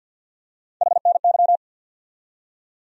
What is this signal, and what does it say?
Morse code, heard as a keyed tone, about 51 wpm, 700 Hz. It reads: HNY
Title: USA Call Sign Postfixes - 3 Letters 50wpm